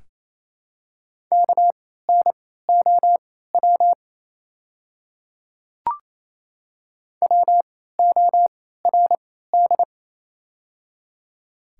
Morse code, heard as a keyed tone, about 28 wpm, spaced as for 16 wpm, 690 Hz. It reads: KNOW WORD
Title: 3x Character Spacing - Top 100 Words - 28wpm